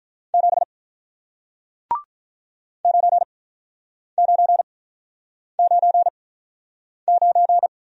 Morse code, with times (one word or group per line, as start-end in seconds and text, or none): 0.3-0.6 7
2.9-7.7 9999